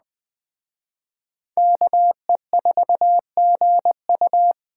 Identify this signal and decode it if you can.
Morse code, heard as a keyed tone, about 25 wpm, 700 Hz. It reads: KE4GU
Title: Speed Racing - USA Call Signs 20wpm